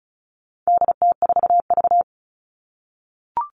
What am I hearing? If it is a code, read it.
Morse code, heard as a keyed tone, about 34 wpm, 700 Hz. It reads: DT4V
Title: All Cut Numbers - 4-Digit 35wpm